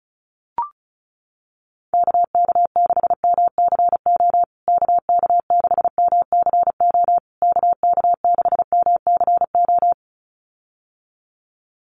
Morse code, heard as a keyed tone, about 34 wpm, 700 Hz. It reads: KK6MCO KK6MCO KK6MCO
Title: USA Call Signs Repeated 3x 35wpm